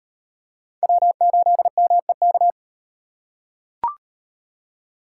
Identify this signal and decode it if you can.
Morse code, heard as a keyed tone, about 38 wpm, 700 Hz. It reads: W8MEK